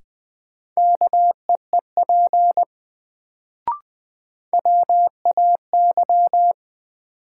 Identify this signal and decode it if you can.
Morse code, heard as a keyed tone, about 20 wpm, 700 Hz. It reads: KEEP WAY
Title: Sets of 2 Words - New Word Drill - 20wpm